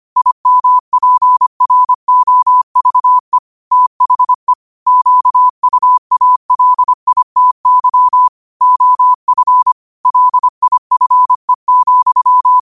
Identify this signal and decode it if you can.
Morse code, heard as a keyed tone, about 25 wpm, 1000 Hz. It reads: IMPROVE THE QUALITY OF LIFE,
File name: qotdfiles / 25wpm